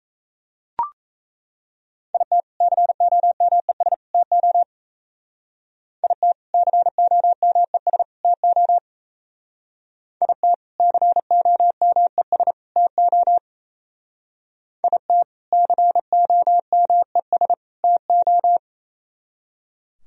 Morse code, heard as a keyed tone, about 35 wpm, 700 Hz. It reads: IT COMES TO IT COMES TO IT COMES TO IT COMES TO
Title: Speed-Racing - Sets of 3 Words 28wpm